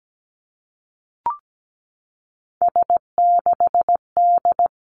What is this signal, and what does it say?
Morse code, heard as a keyed tone, about 17 wpm, 700 Hz. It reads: S6D
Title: Three Challenging Letter-Number Combinations 17wpm